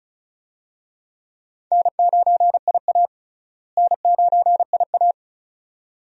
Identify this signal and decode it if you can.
Morse code, heard as a keyed tone, about 36 wpm, 700 Hz. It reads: N9IA N9IA
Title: USA Call Signs - 4 Characters - Repeated 2x - 35wpm